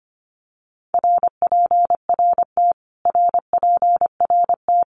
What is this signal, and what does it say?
Morse code, heard as a keyed tone, about 25 wpm, 700 Hz. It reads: RPRT RPRT
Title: Abbreviations Repeated 3x - 25wpm